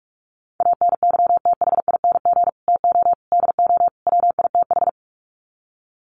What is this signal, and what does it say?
Morse code, heard as a keyed tone, about 44 wpm, 700 Hz. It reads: ANYTHING TO DO WITH
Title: Sets of 4 Words 45wpm